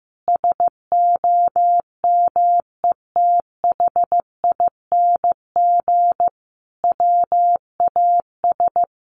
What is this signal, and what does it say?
Morse code, heard as a keyed tone, about 15 wpm, 700 Hz. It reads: SOMETHING WAS
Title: Sentences from Top 1300 Words 15wpm